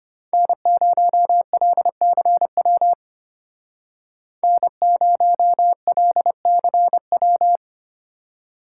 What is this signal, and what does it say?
Morse code, heard as a keyed tone, about 30 wpm, 700 Hz. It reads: N0LCW N0LCW
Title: Speed Racing - USA Call Signs 25wpm